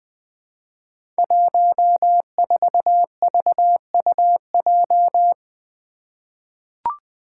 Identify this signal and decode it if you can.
Morse code, heard as a keyed tone, about 20 wpm, 700 Hz. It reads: 14VUJ
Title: Five Challenging Letter-Number Combinations 20wpm